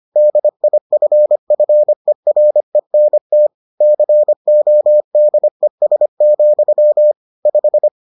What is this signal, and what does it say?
Morse code, heard as a keyed tone, about 25 wpm, 600 Hz. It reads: DIFFERENT CODES, 5